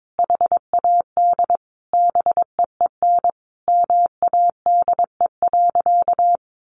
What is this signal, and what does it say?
Morse code, heard as a keyed tone, about 22 wpm, 700 Hz. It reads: HAD BEEN MADE.
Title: Sentences from Top 1400 Words 22wpm